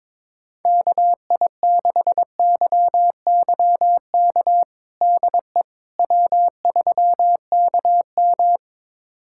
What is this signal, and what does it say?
Morse code, heard as a keyed tone, about 22 wpm, 700 Hz. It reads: KI6YYK DE W3KM
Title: USA Callsign DE Callsign 22wpm